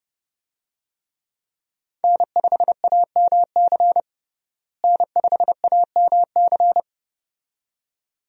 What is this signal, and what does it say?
Morse code, heard as a keyed tone, about 32 wpm, 700 Hz. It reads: N5AMC N5AMC
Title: USA Call Signs - 5 Characters - Repeated 2x - 30wpm